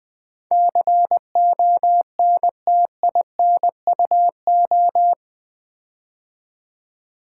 Morse code, heard as a keyed tone, about 20 wpm, 700 Hz. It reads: CONTINUO
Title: Build A Word - 4 Letters At A Time - 10-Letter Words - 20wpm